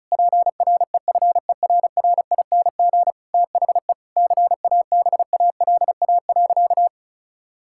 Morse code, heard as a keyed tone, about 36 wpm, 700 Hz. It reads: PREFERRING THE CABALA.
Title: Pangram Challenge v2 - 35wpm